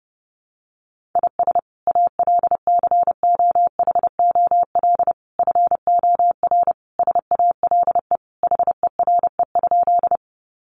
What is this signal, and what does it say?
Morse code, heard as a keyed tone, about 30 wpm, 700 Hz. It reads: IS ALCOHOL FOR SALE HERE?
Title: Sentences from Top 1400 Words 30wpm